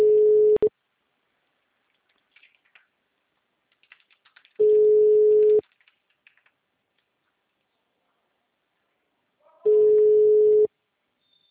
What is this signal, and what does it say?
Ringback tone (Europe, 425 Hz).
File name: out- -2011- - - .242